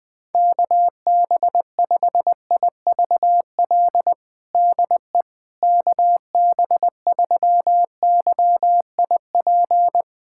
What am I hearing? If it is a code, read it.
Morse code, heard as a keyed tone, about 21 wpm, 700 Hz. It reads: KB5IVL DE KB3YIP
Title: USA Callsign DE Callsign - 6 Characters - 20wpm